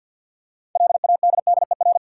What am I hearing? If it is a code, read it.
Morse code, heard as a keyed tone, about 51 wpm, 700 Hz. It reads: LADDER
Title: Six Letter Words - 50wpm